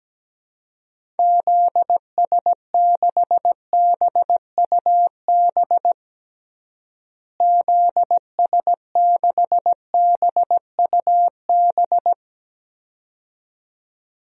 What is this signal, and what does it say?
Morse code, heard as a keyed tone, about 17 wpm, 700 Hz. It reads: ZS6BUB ZS6BUB
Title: DXCC-100 Call Signs - 6 Characters - Repeated 2x - 17wpm